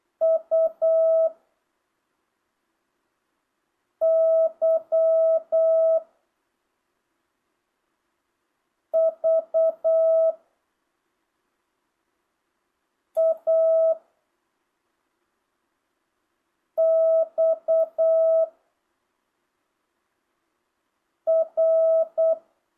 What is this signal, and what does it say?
Morse code, heard as a keyed tone, about 8 wpm, 660 Hz. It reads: U Y V A X R
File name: jh8cbh / 2025cw